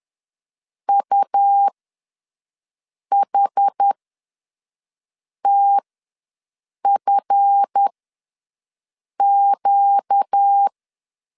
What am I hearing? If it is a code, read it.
Morse code, heard as a keyed tone, about 10 wpm, spaced as for 5 wpm, 800 Hz. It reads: UHTFQ
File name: LEZIONI / 7 GF9Z / 7 GF9 ws 10wpm 4034